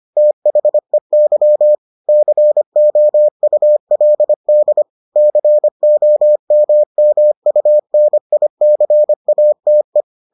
Morse code, heard as a keyed tone, about 25 wpm, 600 Hz. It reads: THEY COULD COMMUNICATE